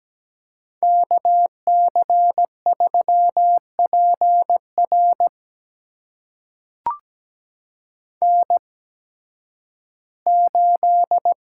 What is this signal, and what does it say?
Morse code, heard as a keyed tone, about 17 wpm, spaced as for 11 wpm, 700 Hz. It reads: KC3PR N8
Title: USA Call Signs - 5 Characters - 8x Character Spacing - 17wpm